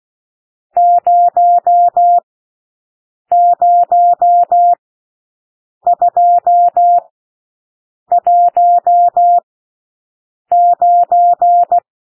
Morse code, heard as a keyed tone, about 16 wpm, spaced as for 6 wpm, 700 Hz. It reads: 00219
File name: SOUND NUMBERS 16 wpm